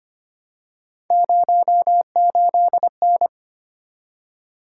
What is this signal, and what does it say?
Morse code, heard as a keyed tone, about 25 wpm, 700 Hz. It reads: 08N
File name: Three Letter-Number Combinations - 25wpm